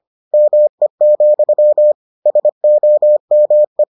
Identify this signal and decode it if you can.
Morse code, heard as a keyed tone, about 25 wpm, 600 Hz. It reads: ME, SOME